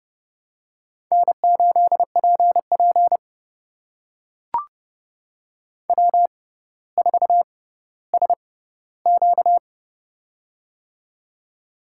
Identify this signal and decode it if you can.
Morse code, heard as a keyed tone, about 30 wpm, spaced as for 15 wpm, 700 Hz. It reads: N8PP W4SQ
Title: USA Call Signs - 4 Characters - 6x Character Spacing - 30wpm